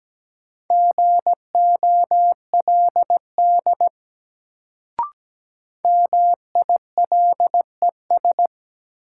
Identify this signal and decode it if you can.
Morse code, heard as a keyed tone, about 17 wpm, 700 Hz. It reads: GOLD MILES
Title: Sentences from Top 1000 Words - Review Part 3 - New Word Drill - 17wpm